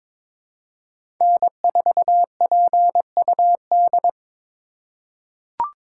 Morse code, heard as a keyed tone, about 22 wpm, 700 Hz. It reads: N4PUD
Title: USA Call Signs - 5 Characters - 2x Character Spacing - 22wpm